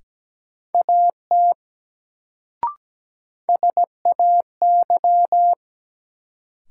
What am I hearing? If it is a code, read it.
Morse code, heard as a keyed tone, about 17 wpm, 700 Hz. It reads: AT SAY
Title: Sets of 2 Words - New Word Drill - 17wpm